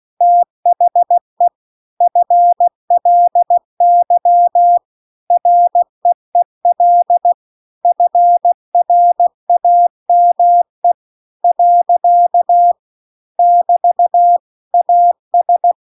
Morse code, heard as a keyed tone, about 16 wpm, 700 Hz. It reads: THE FLY REEL FRAME . <BT> AS
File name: CW-The-Big-Twoheaded-River-2-16WPM